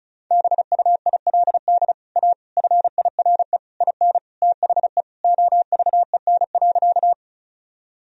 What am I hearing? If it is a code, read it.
Morse code, heard as a keyed tone, about 36 wpm, 700 Hz. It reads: BUILD A FIRE IN THE OVEN.
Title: Sentences from Top 1000 Words - Review Part 4 - 35wpm